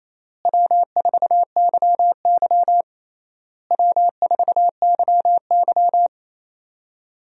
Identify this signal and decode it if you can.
Morse code, heard as a keyed tone, about 27 wpm, 700 Hz. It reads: W4YY W4YY
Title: USA Call Signs - 4 Characters - Repeated 3x - 28wpm